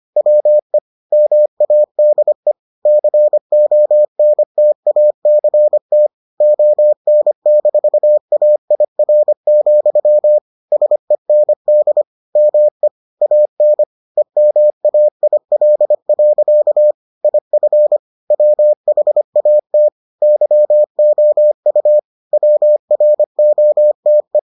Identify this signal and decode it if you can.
Morse code, heard as a keyed tone, about 25 wpm, 600 Hz. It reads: WE MADE CONTACT ON-AIR, SEND ME AN EMAIL. IF WHAT YOU WROTE